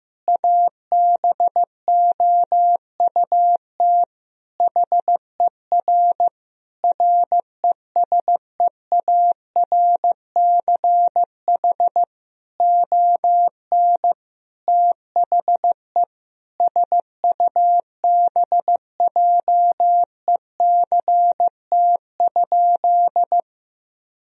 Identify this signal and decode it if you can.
Morse code, heard as a keyed tone, about 16 wpm, 700 Hz. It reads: ABOUT HER RESEARCH ON THE SUBJECT?
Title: Sentences from Top 1000 Words - Review Part 4 - 15wpm